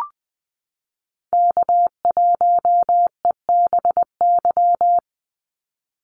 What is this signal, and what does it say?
Morse code, heard as a keyed tone, about 20 wpm, 700 Hz. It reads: K1EBY
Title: Call Signs Repeated 2x - 20wpm